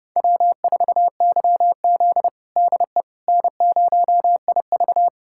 Morse code, heard as a keyed tone, about 32 wpm, 700 Hz. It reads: W4YZ DE N0IV
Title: USA Callsign DE Callsign - 4 Characters - 30wpm